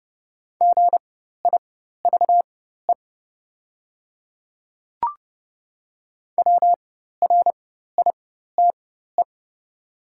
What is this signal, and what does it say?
Morse code, heard as a keyed tone, about 30 wpm, spaced as for 14 wpm, 700 Hz. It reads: GIVE WRITE
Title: 4x Character Spacing - Top 200 Words - 30wpm